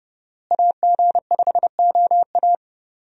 Morse code, heard as a keyed tone, about 32 wpm, 700 Hz. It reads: AG5OA
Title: USA Call Signs - 5 Characters - Repeated 2x - 30wpm